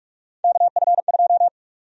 Morse code, heard as a keyed tone, about 47 wpm, 700 Hz. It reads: KF2
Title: USA Call Sign Prefixes - 45wpm